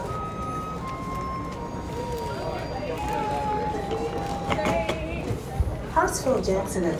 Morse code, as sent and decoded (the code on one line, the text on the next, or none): none